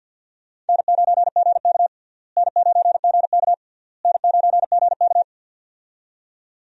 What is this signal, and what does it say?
Morse code, heard as a keyed tone, about 46 wpm, 700 Hz. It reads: N9GK N9GK N9GK
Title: USA Call Signs - 4 Characters - Repeated 3x - 50wpm